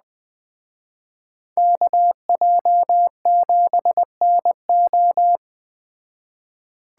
Morse code, heard as a keyed tone, about 21 wpm, 700 Hz. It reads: KJ7NO